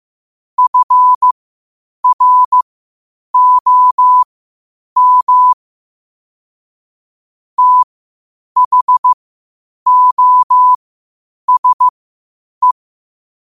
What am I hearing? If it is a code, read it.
Morse code, heard as a keyed tone, about 15 wpm, spaced as for 8 wpm, 1000 Hz. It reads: FROM THOSE
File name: qotdfiles / 8wpm